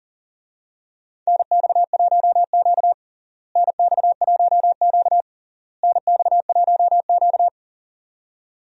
Morse code, heard as a keyed tone, about 37 wpm, 700 Hz. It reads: NX1Q NX1Q NX1Q
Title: USA Call Signs - 4 Characters - Repeated 3x - 40wpm